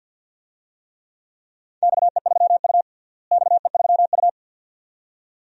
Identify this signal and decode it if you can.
Morse code, heard as a keyed tone, about 51 wpm, 700 Hz. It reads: XE3U XE3U
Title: DXCC-100 Call Signs - 4 Characters - Repeated 2x - 50wpm